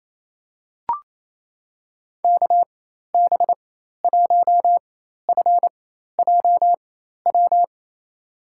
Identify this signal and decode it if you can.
Morse code, heard as a keyed tone, about 29 wpm, spaced as for 13 wpm, 700 Hz. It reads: KB1FJW
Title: USA Call Signs - 6 Characters - 4x Character Spacing - 28wpm